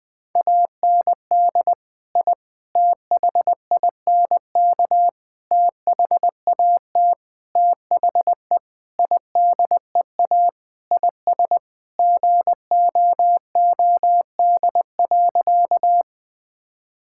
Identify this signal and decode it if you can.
Morse code, heard as a keyed tone, about 20 wpm, 700 Hz. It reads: AND I THINK THAT THE IDEA IS GOOD.
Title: Sentences with 5-Word Phrases - 20wpm